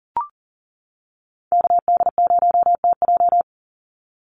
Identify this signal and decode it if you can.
Morse code, heard as a keyed tone, about 40 wpm, 700 Hz. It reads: KD0TJ
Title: USA Call Signs 40wpm